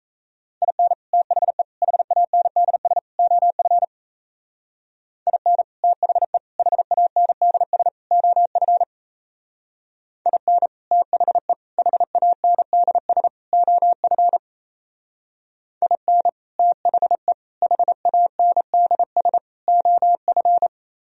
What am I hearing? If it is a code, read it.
Morse code, heard as a keyed tone, about 35 wpm, 700 Hz. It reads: IN THE HANDS OF IN THE HANDS OF IN THE HANDS OF IN THE HANDS OF
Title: Speed-Racing - Sets of 4 Words 28wpm